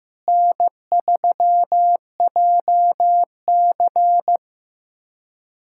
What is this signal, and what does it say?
Morse code, heard as a keyed tone, about 15 wpm, 700 Hz. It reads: N3JC
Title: USA Callsign DE Callsign - 4 Characters - 15wpm